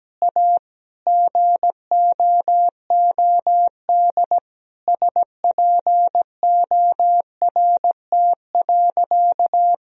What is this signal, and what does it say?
Morse code, heard as a keyed tone, about 17 wpm, 700 Hz. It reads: A GOOD SPORT.
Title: Sentences from Top 1000 Words - Review Part 4 - 17wpm